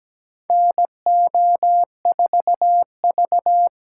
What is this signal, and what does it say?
Morse code, heard as a keyed tone, about 17 wpm, 700 Hz. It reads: NO4V
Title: USA Call Signs - 2x Character Spacing - 17wpm